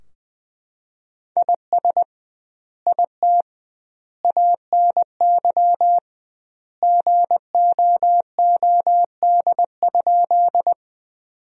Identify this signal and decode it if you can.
Morse code, heard as a keyed tone, about 20 wpm, 700 Hz. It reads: IS IT ANY GOOD?
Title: 2x Word-Spacing - Sentences from Top 200 Words - 20wpm